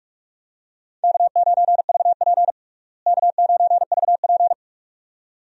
Morse code, heard as a keyed tone, about 41 wpm, 700 Hz. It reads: K9VP K9VP
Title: USA Call Signs - 4 Characters - Repeated 2x - 45wpm